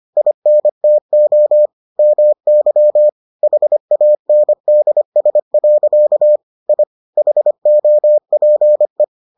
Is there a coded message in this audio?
Yes: Morse code from 0.2 to 9.0 s, heard as a keyed tone, about 25 wpm, 600 Hz. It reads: INTO MY HANDS. I HOPE